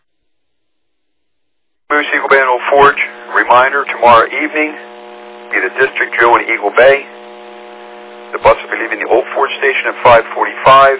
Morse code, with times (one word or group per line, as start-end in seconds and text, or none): none